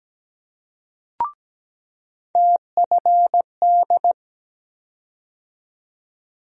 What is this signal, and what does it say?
Morse code, heard as a keyed tone, about 17 wpm, 700 Hz. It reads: TFD
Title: USA Call Sign Postfixes - 3 Letters 17wpm